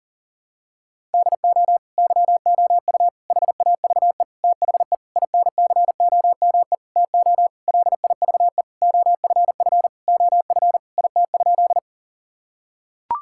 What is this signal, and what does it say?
Morse code, heard as a keyed tone, about 37 wpm, 700 Hz. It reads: DO YOU HAVE THE INCOME TO LIVE OFF OF IT?
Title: Sentences from Top 1000 Words - Review Part 5 - 40wpm